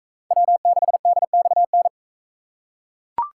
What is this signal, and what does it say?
Morse code, heard as a keyed tone, about 39 wpm, 700 Hz. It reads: W6DXN